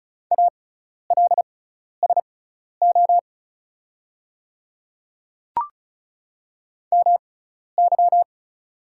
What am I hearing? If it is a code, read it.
Morse code, heard as a keyed tone, about 36 wpm, spaced as for 12 wpm, 700 Hz. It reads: ALSO MY
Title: 6x Character Spacing - Top 500 Words - 35wpm